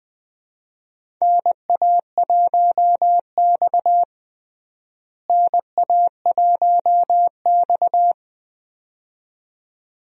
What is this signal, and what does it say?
Morse code, heard as a keyed tone, about 20 wpm, 700 Hz. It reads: NA1X NA1X
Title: USA Call Signs - 4 Characters - Repeated 2x - 20wpm